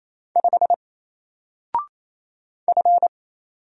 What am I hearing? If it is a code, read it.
Morse code, heard as a keyed tone, about 27 wpm, 700 Hz. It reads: 5 F
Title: Single Character (Letter:Number:Punctuation:Prosign) 28wpm